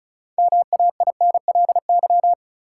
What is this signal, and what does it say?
Morse code, heard as a keyed tone, about 36 wpm, 700 Hz. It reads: MAINLY
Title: Sentences from Top 1400 Words - New Word Drill - 35wpm